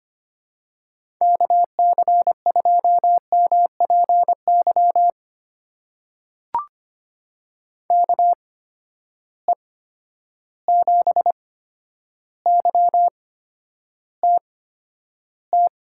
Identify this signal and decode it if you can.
Morse code, heard as a keyed tone, about 25 wpm, spaced as for 11 wpm, 700 Hz. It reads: KC2MPY KE7YTT
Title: USA Call Signs - 6 Characters - 8x Character Spacing - 25wpm